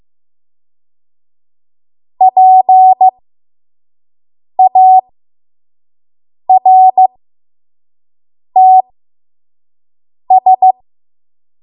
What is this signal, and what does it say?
Morse code, heard as a keyed tone, about 15 wpm, spaced as for 5 wpm, 750 Hz. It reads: PARTS
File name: Morse / 05WPM